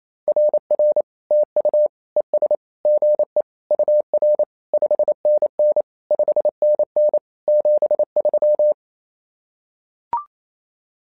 Morse code, heard as a keyed tone, about 27 wpm, 600 Hz. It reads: RR TU ES GE UR 5NN 5NN 73
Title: SOTA QSO Practice 28wpm